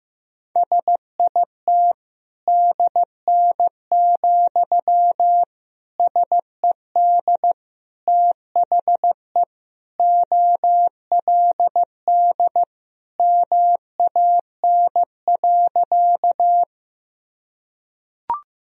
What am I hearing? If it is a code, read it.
Morse code, heard as a keyed tone, about 16 wpm, 700 Hz. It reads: SIT DN, SED THE OLD MAN.
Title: Sentences from Top 500 Words and Abbreviations 15wpm